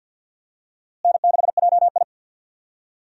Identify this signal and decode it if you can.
Morse code, heard as a keyed tone, about 52 wpm, 700 Hz. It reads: N6JI
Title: USA Call Signs - 4 Characters - Repeated 3x - 50wpm